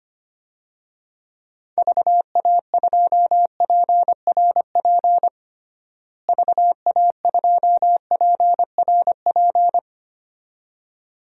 Morse code, heard as a keyed tone, about 25 wpm, 700 Hz. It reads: VA2PRP VA2PRP
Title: DXCC-100 Call Signs - 6 Characters - Repeated 2x - 25wpm